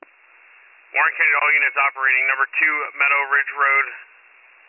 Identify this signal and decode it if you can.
Emergency Alert System (SAME) tones, heard as harsh buzzing bursts, with speech. It is heard from 1.1 to 4.1 s.